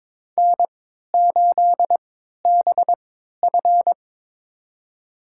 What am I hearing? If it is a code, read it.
Morse code, heard as a keyed tone, about 22 wpm, spaced as for 13 wpm, 700 Hz. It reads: N8BF
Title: USA Call Signs - 4 Characters - 3x Character Spacing - 22wpm